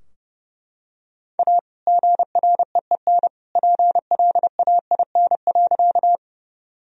Morse code, heard as a keyed tone, about 32 wpm, 700 Hz. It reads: A GREEN PLAIN.
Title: Sentences from Top 1000 Words - Review Part 2 - 30wpm